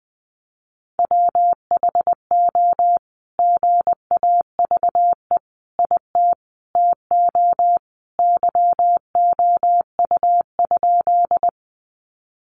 Morse code, heard as a keyed tone, about 20 wpm, 700 Hz. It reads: WHO GAVE IT TO YOU?
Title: Sentences from Top 500 Words 20wpm